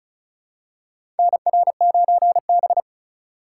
Morse code, heard as a keyed tone, about 36 wpm, 700 Hz. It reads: NR9B
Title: USA Call Signs - 4 Characters - 3x Character Spacing - 35wpm